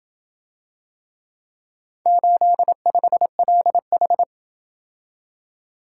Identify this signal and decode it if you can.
Morse code, heard as a keyed tone, about 27 wpm, 700 Hz. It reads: 85LH